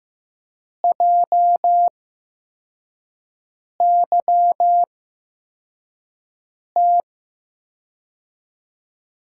Morse code, heard as a keyed tone, about 15 wpm, spaced as for 4 wpm, 700 Hz. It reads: JYT
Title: USA Call Signs - 8x Character Spacing - 15wpm